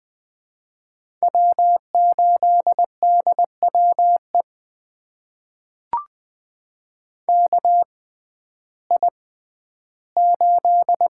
Morse code, heard as a keyed tone, about 20 wpm, spaced as for 12 wpm, 700 Hz. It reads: W8DWE KI8
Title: USA Call Signs - 5 Characters - 6x Character Spacing - 20wpm